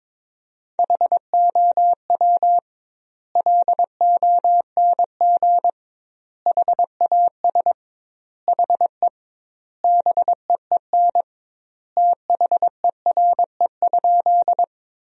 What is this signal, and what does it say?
Morse code, heard as a keyed tone, about 22 wpm, 700 Hz. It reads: HOW LONG HAS HE BEEN THERE?
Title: 2x Word-Spacing - Sentences from Top 100 Words - 22wpm